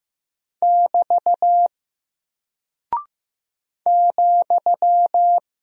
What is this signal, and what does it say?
Morse code, heard as a keyed tone, about 15 wpm, 700 Hz. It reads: <BT> ,
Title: Single Character (Letter:Number:Punctuation:Prosign) 15wpm